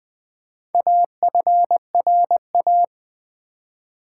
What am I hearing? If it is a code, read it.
Morse code, heard as a keyed tone, about 20 wpm, 700 Hz. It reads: AFRA
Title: Build A Word - 2 Letters At A Time - 6-Letter Words - 20wpm